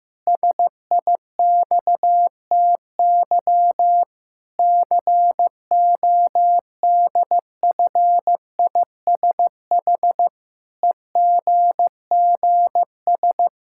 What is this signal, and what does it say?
Morse code, heard as a keyed tone, about 15 wpm, 700 Hz. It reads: SIXTY CODFISH EGGS
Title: Pangram Challenge v2 - 15wpm